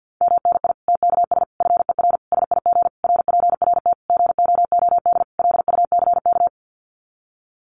Morse code, heard as a keyed tone, about 48 wpm, 700 Hz. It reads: MNI TKS FER SIG RPRT GOOD LUCK
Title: Common QSO Phrases 50wpm